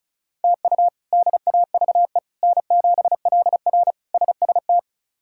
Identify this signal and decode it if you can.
Morse code, heard as a keyed tone, about 36 wpm, 700 Hz. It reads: TU DAVE N7LR SST
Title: K1USN SST QSO Practice - 35wpm